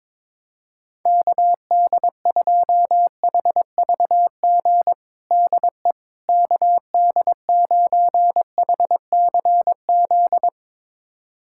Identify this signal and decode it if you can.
Morse code, heard as a keyed tone, about 22 wpm, 700 Hz. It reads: KD2HVG DE KD9HCZ
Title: USA Callsign DE Callsign - 6 Characters - 22wpm